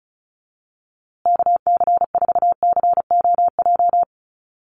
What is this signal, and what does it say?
Morse code, heard as a keyed tone, about 34 wpm, 700 Hz. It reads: KC4COJ
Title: USA Call Signs Repeated 2x 35wpm